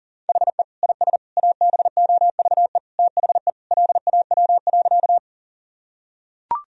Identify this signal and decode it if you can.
Morse code, heard as a keyed tone, about 37 wpm, 700 Hz. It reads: HE IS ABOVE THE LAW.
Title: Sentences from Top 1000 Words - Review Part 3 - 40wpm